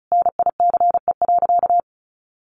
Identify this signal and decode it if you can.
Morse code, heard as a keyed tone, about 34 wpm, 700 Hz. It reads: NICE.
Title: Easy Phrases 35wpm